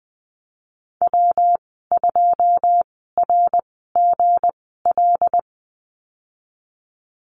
Morse code, heard as a keyed tone, about 20 wpm, spaced as for 15 wpm, 700 Hz. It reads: W2RGL
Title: 161 Lesson 26g - Letter g - Call Signs 20wpm